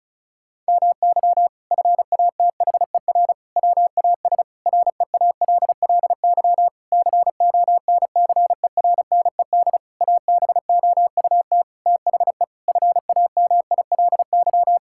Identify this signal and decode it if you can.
Morse code, heard as a keyed tone, about 34 wpm, 700 Hz. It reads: MY FATHER WAS REALLY CONCERNED ABOUT THE FAMILY
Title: Sentences from Top 1000 Words - Review Part 6 - 35wpm